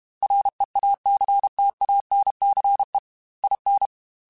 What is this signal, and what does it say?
Morse code, heard as a keyed tone, about 32 wpm, 800 Hz. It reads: REACTANCE IN
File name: Technical Subjects / Technical Subjects-mp3s / XC @30